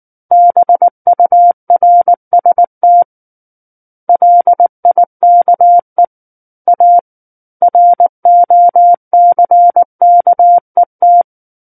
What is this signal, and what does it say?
Morse code, heard as a keyed tone, about 19 wpm, 700 Hz. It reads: BURST LIKE A ROCKET
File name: Book One / 10-Chapter 10 - 19 WPM